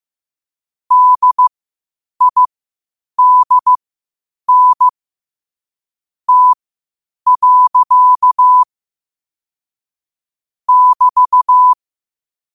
Morse code, heard as a keyed tone, about 15 wpm, spaced as for 8 wpm, 1000 Hz. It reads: DIDN T. <BT>
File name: qotdfiles / 8wpm